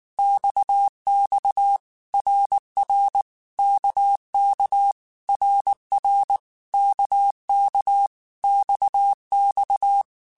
Morse code, heard as a keyed tone, about 19 wpm, 800 Hz. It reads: XX RR KK RR KK XX
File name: Basic Letter Groups By7Groups / RKPX / RKPX@18